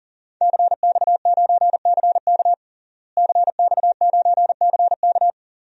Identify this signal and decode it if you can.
Morse code, heard as a keyed tone, about 40 wpm, 700 Hz. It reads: CX9CK CX9CK
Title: DXCC-100 Call Signs - 5 Characters - Repeated 2x - 40wpm